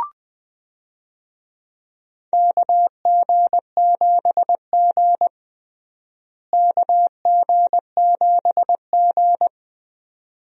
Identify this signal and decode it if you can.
Morse code, heard as a keyed tone, about 20 wpm, 700 Hz. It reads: KG7G KG7G
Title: USA Call Signs - 4 Characters - Repeated 2x - 20wpm